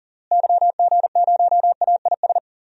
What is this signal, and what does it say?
Morse code, heard as a keyed tone, about 40 wpm, 700 Hz. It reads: YG0AIS
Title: DXCC-100 Call Signs - 6 Characters - Repeated 3x - 40wpm